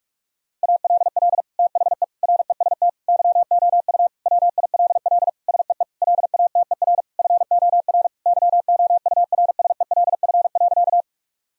Morse code, heard as a keyed tone, about 46 wpm, 700 Hz. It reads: ALL THE REST YOU WILL SEE LATER FOR YOURSELF.
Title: Sentences from Top 1000 Words - Review Part 4 - 45wpm